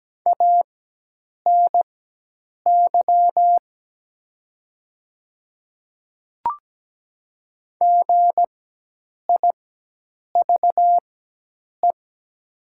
Morse code, heard as a keyed tone, about 17 wpm, spaced as for 8 wpm, 700 Hz. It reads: ANY GIVE
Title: 4x Character Spacing - Top 500 Words - 17wpm